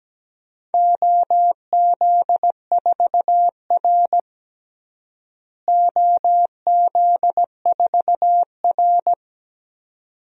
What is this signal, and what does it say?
Morse code, heard as a keyed tone, about 17 wpm, 700 Hz. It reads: OZ4R OZ4R
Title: DXCC-100 Call Signs - 4 Characters - Repeated 2x - 17wpm